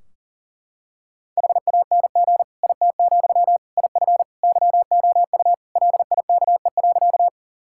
Morse code, heard as a keyed tone, about 37 wpm, 700 Hz. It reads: HANG IT, IF YOU LIKE.
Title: Sentences from Top 1600 Words 40wpm